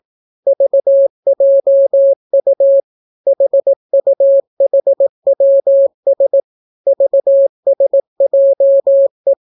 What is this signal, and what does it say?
Morse code, heard as a keyed tone, about 18 wpm, 550 Hz. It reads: VJU HUHWS VSJE